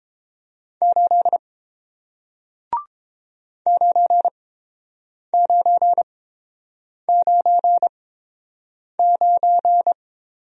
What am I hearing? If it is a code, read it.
Morse code, heard as a keyed tone, about 29 wpm, spaced as for 7 wpm, 700 Hz. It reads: 8 9999